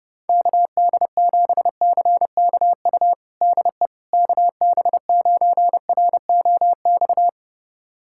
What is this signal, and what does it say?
Morse code, heard as a keyed tone, about 32 wpm, 700 Hz. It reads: KD7CKU DE KB9ROX
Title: USA Callsign DE Callsign 30wpm